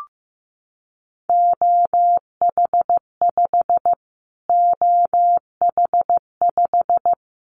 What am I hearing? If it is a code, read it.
Morse code, heard as a keyed tone, about 15 wpm, 700 Hz. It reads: OH5 OH5
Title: Callsign Prefixes for Top 25 DXCC - Repeated 3x - 15wpm